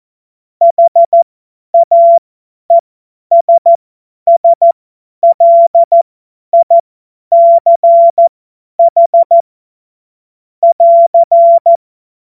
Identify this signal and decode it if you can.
Morse code, heard as a keyed tone, about 14 wpm, spaced as for 10 wpm, 680 Hz. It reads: HAESSLICH <AR>
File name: Morsekurs / Witze / Witz 09 Januar 2024 50 F70